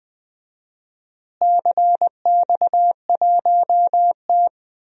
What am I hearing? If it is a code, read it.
Morse code, heard as a keyed tone, about 20 wpm, 700 Hz. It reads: CX1T
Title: DXCC-100 Call Signs - 4 Characters - Repeated 2x - 20wpm